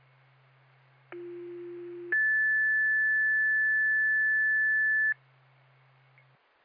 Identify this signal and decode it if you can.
Two-tone page, heard as a one-second tone, then a three-second tone, about 350 then 1750 Hz. It is heard from 1.1 to 5.1 s.